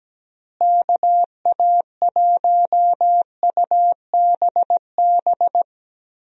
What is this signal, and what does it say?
Morse code, heard as a keyed tone, about 17 wpm, 700 Hz. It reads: KA1UBB
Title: USA Call Signs - 6 Characters - Repeated 2x - 17wpm